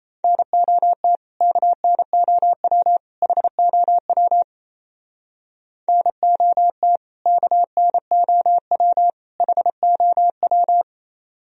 Morse code, heard as a keyed tone, about 36 wpm, 700 Hz. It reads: NOT KNOW HOW NOT KNOW HOW